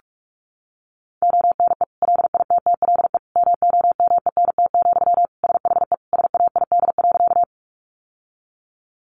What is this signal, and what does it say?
Morse code, heard as a keyed tone, about 44 wpm, 700 Hz. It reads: ONE LITTLE MOMENT, SHE SAID.
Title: Sentences from Top 1000 Words 45wpm